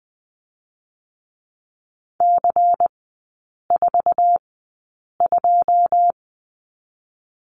Morse code, heard as a keyed tone, about 20 wpm, spaced as for 8 wpm, 700 Hz. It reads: C42
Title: 107 Lesson 18e - Letter m - Three Characters 20wpm